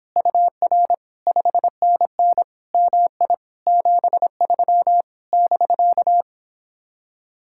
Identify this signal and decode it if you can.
Morse code, heard as a keyed tone, about 31 wpm, 700 Hz. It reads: UR 5NN MI 73 <BK>